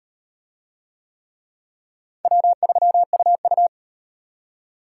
Morse code, heard as a keyed tone, about 38 wpm, 700 Hz. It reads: W3UU